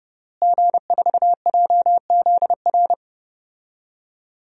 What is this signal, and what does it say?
Morse code, heard as a keyed tone, about 30 wpm, 700 Hz. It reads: G4JZR
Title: DXCC-100 Call Signs - 5 Characters - Repeated 3x - 30wpm